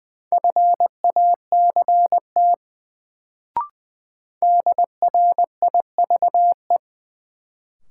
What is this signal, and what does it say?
Morse code, heard as a keyed tone, about 20 wpm, 700 Hz. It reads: FACT DRIVE
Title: Sentences from Top 500 Words - New Word Drill - 20wpm